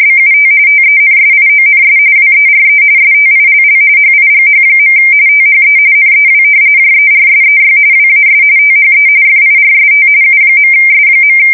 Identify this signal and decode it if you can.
RTTY, about 45.45 baud. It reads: BUT TENSIONS WERE STILL HIGH. THE NEW JERSEY WAS THE FIRST U.S.
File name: fsk170 / battleshipnj